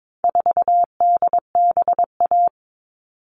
Morse code, heard as a keyed tone, about 22 wpm, 700 Hz. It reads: 4DBA
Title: All Cut Numbers - 4-Digit 22wpm